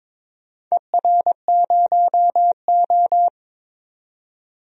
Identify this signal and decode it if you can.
Morse code, heard as a keyed tone, about 21 wpm, 700 Hz. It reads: ER0O
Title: DXCC-100 Call Signs - 4 Characters - 22wpm